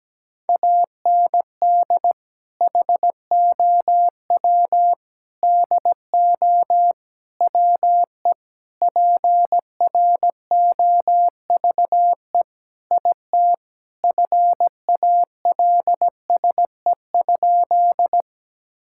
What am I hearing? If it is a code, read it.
Morse code, heard as a keyed tone, about 17 wpm, 700 Hz. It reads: AND HOW DO WE PROVE IT FALSE?
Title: Sentences from Top 1600 Words 17wpm